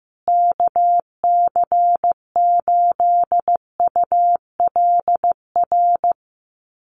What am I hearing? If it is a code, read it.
Morse code, heard as a keyed tone, about 15 wpm, 700 Hz. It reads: KC8ULR
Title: USA Call Signs Repeated 2x 15wpm